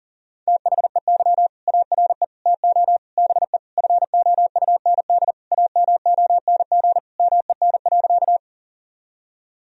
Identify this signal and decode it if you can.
Morse code, heard as a keyed tone, about 40 wpm, 700 Hz. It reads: THEY ARE TO BE FOUND AMONG MEN.
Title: Sentences from Top 1000 Words - Review Part 3 - 40wpm